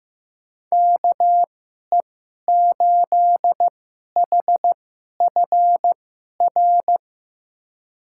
Morse code, heard as a keyed tone, about 15 wpm, spaced as for 11 wpm, 700 Hz. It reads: KE8HFR
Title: USA Call Signs - 6 Characters - 2x Character Spacing - 15wpm